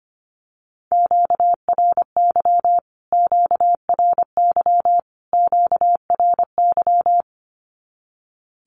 Morse code, heard as a keyed tone, about 25 wpm, 700 Hz. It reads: QRY QRY QRY
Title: Q-Codes Repeated 3x 25wpm